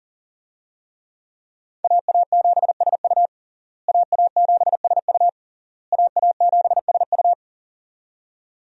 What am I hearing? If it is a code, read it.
Morse code, heard as a keyed tone, about 40 wpm, 700 Hz. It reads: AA7SU AA7SU AA7SU
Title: USA Call Signs - 5 Characters - Repeated 3x - 40wpm